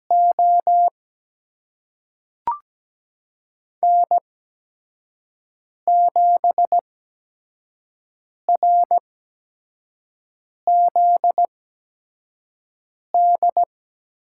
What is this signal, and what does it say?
Morse code, heard as a keyed tone, about 17 wpm, spaced as for 5 wpm, 700 Hz. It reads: O N7RZD
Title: USA Call Signs - 5 Characters - 8x Character Spacing - 17wpm